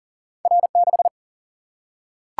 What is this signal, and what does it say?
Morse code, heard as a keyed tone, about 39 wpm, 700 Hz. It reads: R6
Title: Single-Letter Single-Number 40wpm